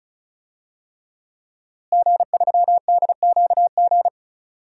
Morse code, heard as a keyed tone, about 36 wpm, 700 Hz. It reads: G3DQG
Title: DXCC-100 Call Signs - 5 Characters - Repeated 3x - 35wpm